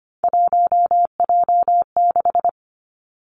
Morse code, heard as a keyed tone, about 25 wpm, 700 Hz. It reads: 1J6
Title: Three Challenging Letter-Number Combinations 25wpm